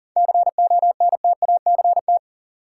Morse code, heard as a keyed tone, about 40 wpm, 700 Hz. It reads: CONTACT
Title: Seven Letter Words - 40wpm